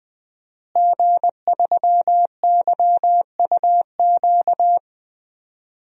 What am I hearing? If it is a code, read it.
Morse code, heard as a keyed tone, about 20 wpm, 700 Hz. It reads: G3YUQ
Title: DXCC-100 Call Signs - 5 Characters - Repeated 3x - 20wpm